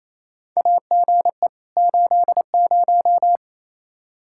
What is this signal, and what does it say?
Morse code, heard as a keyed tone, about 27 wpm, 700 Hz. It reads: AGE 80
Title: K1USN SST QSO Practice 28wpm